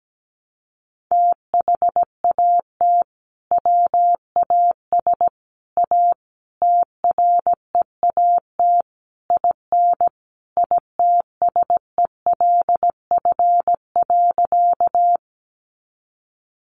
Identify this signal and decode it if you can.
Morse code, heard as a keyed tone, about 17 wpm, 700 Hz. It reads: THAT WAS A TREAT IN ITSELF.
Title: Sentences from Top 1300 Words 17wpm